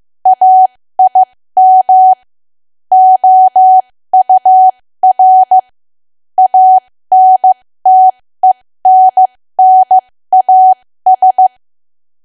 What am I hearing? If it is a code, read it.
Morse code, heard as a keyed tone, about 15 wpm, spaced as for 13 wpm, 750 Hz. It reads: AIM OUR ANTENNAS
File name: cwa / W1AW-Level2 / 13WPM